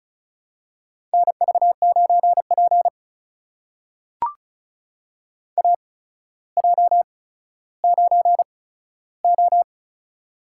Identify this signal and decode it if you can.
Morse code, heard as a keyed tone, about 34 wpm, spaced as for 15 wpm, 700 Hz. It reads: NV9P AJ9O
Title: USA Call Signs - 4 Characters - 8x Character Spacing - 35wpm